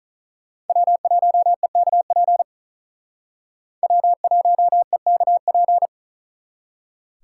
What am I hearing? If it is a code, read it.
Morse code, heard as a keyed tone, about 42 wpm, 700 Hz. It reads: W1EKP W1EKP
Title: Speed Racing - USA 1x3 Call Signs 35wpm